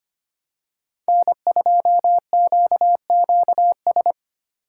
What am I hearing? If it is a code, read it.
Morse code, heard as a keyed tone, about 25 wpm, 700 Hz. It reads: N2QQS
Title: USA Call Signs - 5 Characters - 8x Character Spacing - 25wpm